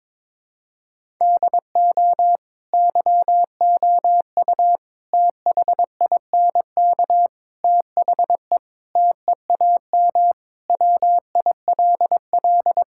Morse code, heard as a keyed tone, about 22 wpm, 700 Hz. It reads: DO YOU THINK THE TEAM WILL
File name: practice / 22wpm / Sentences with 4-Word Phrases - 22wpm